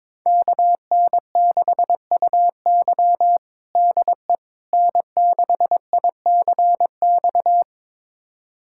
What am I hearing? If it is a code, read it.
Morse code, heard as a keyed tone, about 22 wpm, 700 Hz. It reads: KN6UY DE N6ICX
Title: USA Callsign DE Callsign - 5 Characters - 22wpm